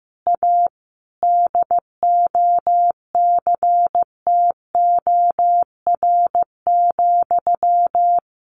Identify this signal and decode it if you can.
Morse code, heard as a keyed tone, about 15 wpm, 700 Hz. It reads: A DOCTOR,
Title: Sentences from Top 1500 Words 15wpm